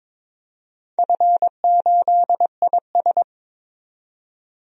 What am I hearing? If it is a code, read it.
Morse code, heard as a keyed tone, about 22 wpm, 690 Hz. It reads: F8IS
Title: DXCC-100 Call Signs - 4 Characters - 22wpm